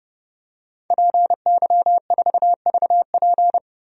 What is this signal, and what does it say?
Morse code, heard as a keyed tone, about 32 wpm, 700 Hz. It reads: PY4VP
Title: DXCC-100 Call Signs - 5 Characters - Repeated 3x - 30wpm